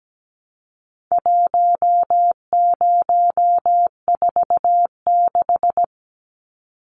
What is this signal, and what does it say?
Morse code, heard as a keyed tone, about 17 wpm, 700 Hz. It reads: 1046
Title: Numbers - 4-Digit 17wpm